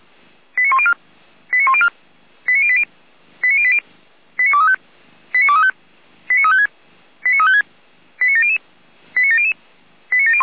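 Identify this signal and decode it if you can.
Five-tone selective call (ZVEI), heard as quick runs of five tones, tones of about 70 ms. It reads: 89104 89104 89080 89080 89235 89235 89356 89356 89700 89700 89701
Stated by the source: Five-tone ZVEI selective call 89020 received by SDR.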